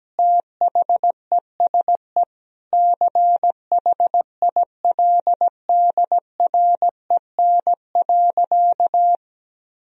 Morse code, heard as a keyed tone, about 17 wpm, 700 Hz. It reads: THESE CHILDREN.
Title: Sentences from Top 1000 Words - Review Part 1 - 17wpm